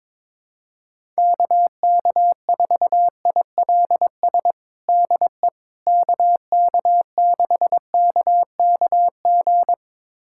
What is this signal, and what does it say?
Morse code, heard as a keyed tone, about 22 wpm, 700 Hz. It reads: KK4ILS DE KK6KKG
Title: USA Callsign DE Callsign - 6 Characters - 22wpm